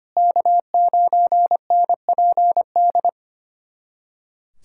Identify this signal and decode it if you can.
Morse code, heard as a keyed tone, about 25 wpm, 700 Hz. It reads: K9NPD
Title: SOTA QSO Practice 25wpm